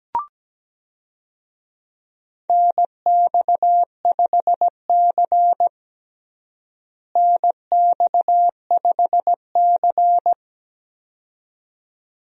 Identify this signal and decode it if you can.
Morse code, heard as a keyed tone, about 17 wpm, 700 Hz. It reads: NX5C NX5C
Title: USA Call Signs - 4 Characters - Repeated 2x - 17wpm